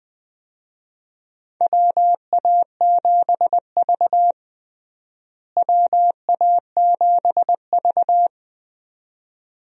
Morse code, heard as a keyed tone, about 20 wpm, 700 Hz. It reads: WA7V WA7V
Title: USA Call Signs - 4 Characters - Repeated 2x - 20wpm